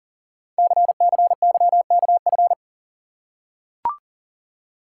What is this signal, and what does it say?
Morse code, heard as a keyed tone, about 39 wpm, 690 Hz. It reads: CCYKF
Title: Code Groups - 5 Letters - 40wpm